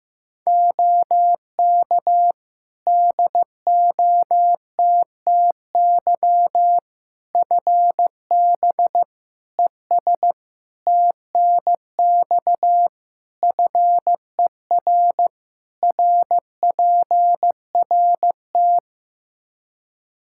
Morse code, heard as a keyed tone, about 15 wpm, 700 Hz. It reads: OK DOTTY FB ES TNX FER RPRT
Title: K1USN SST QSO Practice 15wpm